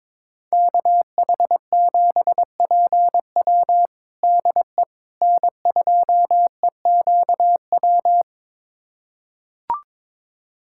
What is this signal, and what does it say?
Morse code, heard as a keyed tone, about 22 wpm, 700 Hz. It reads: KH7PW DE N2EQW
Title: USA Callsign DE Callsign - 5 Characters - 22wpm